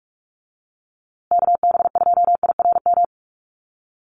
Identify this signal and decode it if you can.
Morse code, heard as a keyed tone, about 44 wpm, 700 Hz. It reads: KB2IRM
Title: USA Call Signs Repeated 2x 45wpm